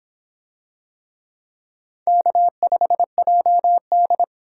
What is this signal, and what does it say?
Morse code, heard as a keyed tone, about 25 wpm, 700 Hz. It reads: K5JD